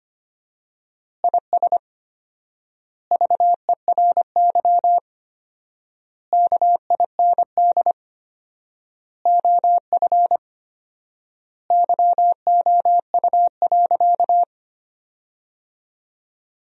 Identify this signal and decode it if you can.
Morse code, heard as a keyed tone, about 25 wpm, 700 Hz. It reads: IS VERY KIND OF YOU.
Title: 4x Word-Spacing - Sentences from Top 200 Words - 25wpm